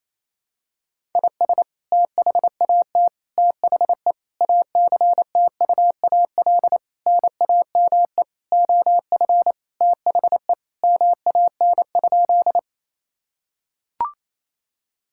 Morse code, heard as a keyed tone, about 28 wpm, 700 Hz. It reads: IS THAT THE ACTUAL NAME OF THE MAN?
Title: Sentences from Top 1000 Words - Review Part 5 - 28wpm